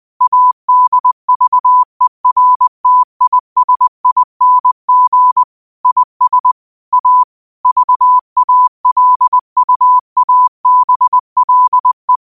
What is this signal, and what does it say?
Morse code, heard as a keyed tone, about 20 wpm, 1000 Hz. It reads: ADVERTISING IS A VALUABLE
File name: qotdfiles / 20wpm